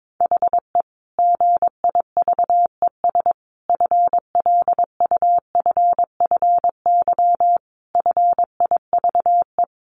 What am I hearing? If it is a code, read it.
Morse code, heard as a keyed tone, about 22 wpm, 700 Hz. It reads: HE GIVES FLUFFY FIVE
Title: Easy Phrases 22wpm